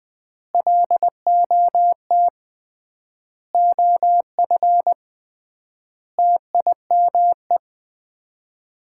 Morse code, heard as a keyed tone, about 20 wpm, 700 Hz. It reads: LOT OF TIME
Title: 3x Word-Spacing - Sets of 5 Words - 20wpm